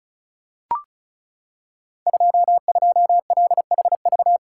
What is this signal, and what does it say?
Morse code, heard as a keyed tone, about 36 wpm, 700 Hz. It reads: 22LHV
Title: Code Groups - 5 Letter:Numbers - 35wpm